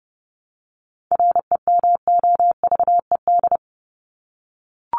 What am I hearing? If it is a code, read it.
Morse code, heard as a keyed tone, about 30 wpm, 700 Hz. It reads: REMOVED
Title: ED Suffix - Top 100 Words with Base Words 30wpm